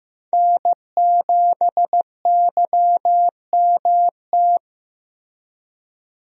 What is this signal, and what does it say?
Morse code, heard as a keyed tone, about 16 wpm, 700 Hz. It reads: N7YMT
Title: USA Call Signs - 5 Characters - Repeated 2x - 15wpm